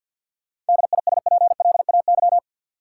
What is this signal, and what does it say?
Morse code, heard as a keyed tone, about 52 wpm, 700 Hz. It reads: DISPLAY
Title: Sentences from Top 1200 Words - New Word Drill - 50wpm